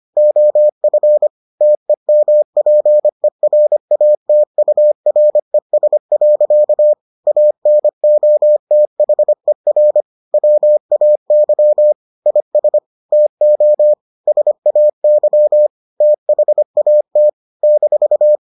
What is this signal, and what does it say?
Morse code, heard as a keyed tone, about 25 wpm, 600 Hz. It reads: OF TEMPERATURES. ANOTHER WAY IS TO SAY THAT -